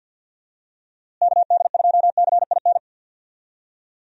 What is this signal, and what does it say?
Morse code, heard as a keyed tone, about 49 wpm, 700 Hz. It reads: KD2CIN
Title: USA Call Signs - 6 Characters - Repeated 3x - 50wpm